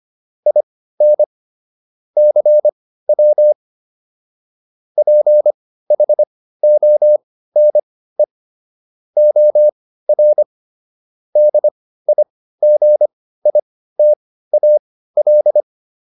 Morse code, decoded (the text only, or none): IN CW PHONE OR DIGITAL